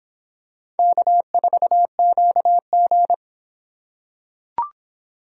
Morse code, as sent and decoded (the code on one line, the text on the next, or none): -.- ....- --.- --.
K4QG